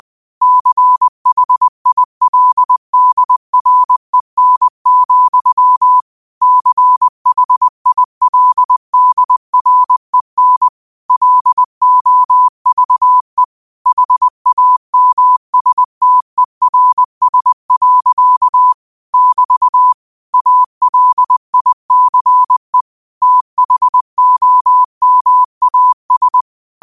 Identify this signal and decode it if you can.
Morse code, heard as a keyed tone, about 20 wpm, 1000 Hz. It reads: CHILDREN, CHILDREN LOVE HAMSTERS. <BT> ALICE THOMAS